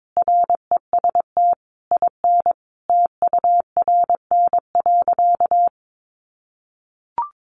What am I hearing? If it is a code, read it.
Morse code, heard as a keyed tone, about 22 wpm, 700 Hz. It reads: REST IN TURN.
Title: Sentences from Top 700 Words 22wpm